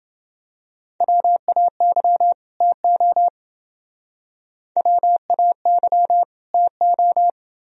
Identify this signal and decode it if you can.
Morse code, heard as a keyed tone, about 28 wpm, 700 Hz. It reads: WAY TO WAY TO